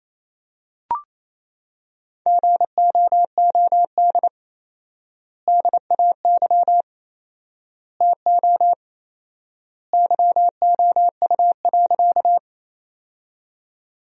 Morse code, heard as a keyed tone, about 28 wpm, 700 Hz. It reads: GOOD DAY TO YOU.
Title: 4x Word-Spacing - Sentences from Top 200 Words - 28wpm